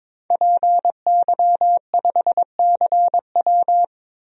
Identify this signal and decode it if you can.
Morse code, heard as a keyed tone, about 21 wpm, 700 Hz. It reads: PY5CW
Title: DXCC-100 Call Signs - 5 Characters - 22wpm